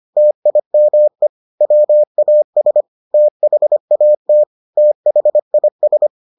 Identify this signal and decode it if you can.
Morse code, heard as a keyed tone, about 25 wpm, 600 Hz. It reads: TIME WAS THAT THIS